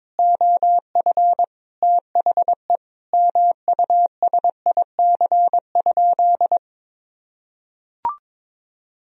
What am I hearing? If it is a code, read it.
Morse code, heard as a keyed tone, about 22 wpm, 700 Hz. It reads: OF THE MUSIC?
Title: Sentences from Top 300 Words and Abbreviations 22wpm